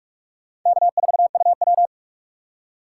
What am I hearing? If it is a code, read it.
Morse code, heard as a keyed tone, about 57 wpm, 700 Hz. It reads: K4UW